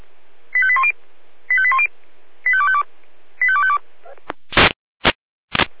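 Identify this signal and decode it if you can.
Five-tone selective call (ZVEI), heard as quick runs of five tones, tones of about 70 ms. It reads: 85810 85810 85252 85252
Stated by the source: ZVEI five-tone selective call 85259, received by SDR.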